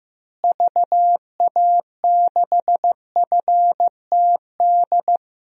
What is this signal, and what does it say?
Morse code, heard as a keyed tone, about 16 wpm, 700 Hz. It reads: VA6FTD
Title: DXCC-100 Call Signs - 6 Characters - Repeated 3x - 15wpm